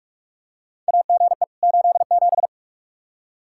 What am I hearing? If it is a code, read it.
Morse code, heard as a keyed tone, about 43 wpm, 700 Hz. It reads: AGE 87
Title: K1USN SST QSO Practice 45wpm